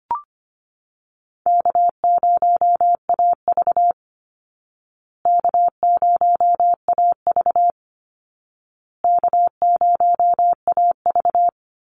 Morse code, heard as a keyed tone, about 25 wpm, 700 Hz. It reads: K0AV K0AV K0AV
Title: Call Signs Repeated 2x - 25wpm